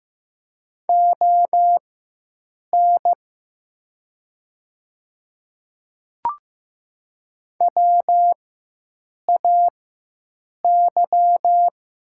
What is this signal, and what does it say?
Morse code, heard as a keyed tone, about 15 wpm, spaced as for 7 wpm, 700 Hz. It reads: ON WAY
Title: 4x Character Spacing - Top 200 Words - 15wpm